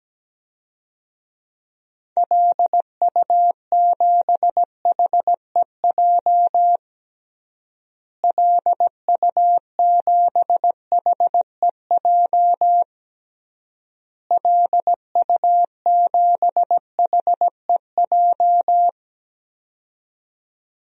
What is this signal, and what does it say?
Morse code, heard as a keyed tone, about 17 wpm, 700 Hz. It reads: LU7HEJ LU7HEJ LU7HEJ
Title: DXCC-100 Call Signs - 6 Characters - Repeated 3x - 17wpm